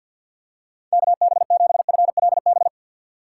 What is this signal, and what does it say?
Morse code, heard as a keyed tone, about 52 wpm, 700 Hz. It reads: KB7FLB
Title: USA Call Signs - 6 Characters - 3x Character Spacing - 50wpm